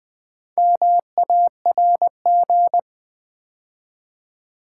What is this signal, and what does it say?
Morse code, heard as a keyed tone, about 20 wpm, 690 Hz. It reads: MARG
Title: Build A Word - 2 Letters At A Time - 6-Letter Words - 20wpm